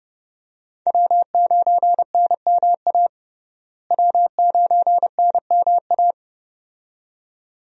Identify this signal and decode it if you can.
Morse code, heard as a keyed tone, about 29 wpm, 700 Hz. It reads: W9NMA W9NMA
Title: USA Call Signs - 5 Characters - Repeated 2x - 30wpm